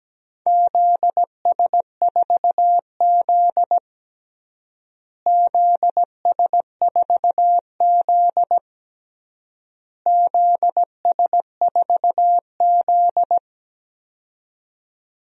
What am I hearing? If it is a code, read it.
Morse code, heard as a keyed tone, about 17 wpm, 700 Hz. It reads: ZS4Z ZS4Z ZS4Z
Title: DXCC-100 Call Signs - 4 Characters - Repeated 3x - 17wpm